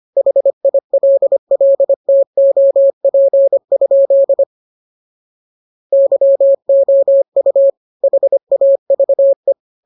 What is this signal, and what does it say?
Morse code, heard as a keyed tone, about 25 wpm, 550 Hz. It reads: HILLTOP? YOU HAVE